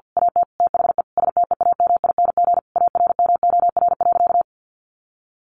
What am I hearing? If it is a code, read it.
Morse code, heard as a keyed tone, about 50 wpm, 700 Hz. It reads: AT THE STEAMING ARMOR.
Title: A Sound Of Thunder 50wpm